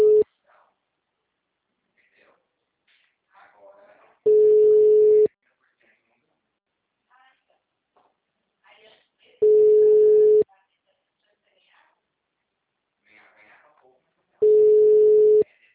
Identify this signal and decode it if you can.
Ringback tone (Europe, 425 Hz), with speech.